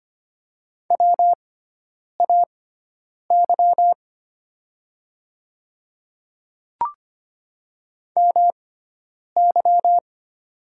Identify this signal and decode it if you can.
Morse code, heard as a keyed tone, about 25 wpm, spaced as for 9 wpm, 700 Hz. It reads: WAY MY
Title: 6x Character Spacing - Top 100 Words - 25wpm